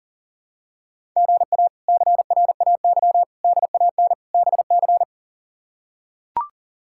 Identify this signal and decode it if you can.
Morse code, heard as a keyed tone, about 37 wpm, 700 Hz. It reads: GA CRAY DAN BC
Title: K1USN SST QSO Practice - 40wpm